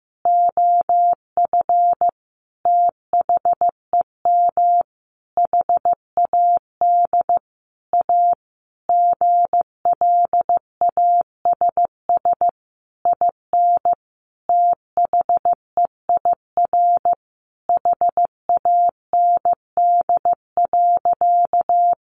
Morse code, heard as a keyed tone, about 15 wpm, 700 Hz. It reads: OF THEM HAD A GLASS IN THEIR HAND.
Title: Sentences from Top 1300 Words 15wpm